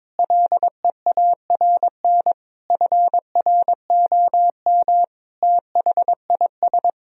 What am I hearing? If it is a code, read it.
Morse code, heard as a keyed tone, about 21 wpm, 700 Hz. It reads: LEARN FROM THIS
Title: Sentences from Top 1000 Words - Review Part 1 - 22wpm